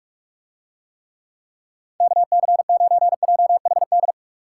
Morse code, heard as a keyed tone, about 47 wpm, 700 Hz. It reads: KC9JHD